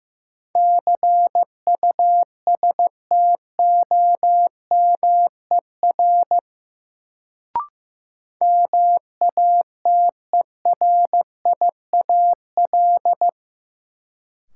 Morse code, heard as a keyed tone, about 15 wpm, 700 Hz. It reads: CUSTOMER MATERIAL
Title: Sentences from Top 1100 Words - New Word Drill Alt - 15wpm